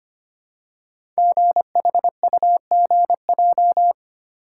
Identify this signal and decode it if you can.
Morse code, heard as a keyed tone, about 25 wpm, 700 Hz. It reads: GHUGJ
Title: Code Groups - 5 Letters - 25wpm